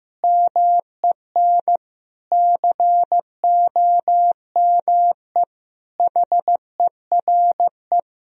Morse code, heard as a keyed tone, about 15 wpm, 700 Hz. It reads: MEN COME HERE
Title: Sentences from Top 1000 Words - Review Part 4 - 15wpm